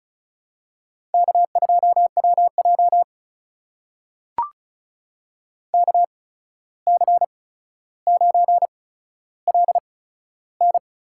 Morse code, heard as a keyed tone, about 36 wpm, spaced as for 14 wpm, 700 Hz. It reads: K2WJ KC9LN
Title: USA Call Signs - 8x Character Spacing - 35wpm